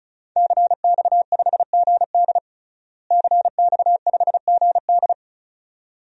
Morse code, heard as a keyed tone, about 36 wpm, 700 Hz. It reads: CX5GD CX5GD
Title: DXCC-100 Call Signs - 5 Characters - Repeated 2x - 35wpm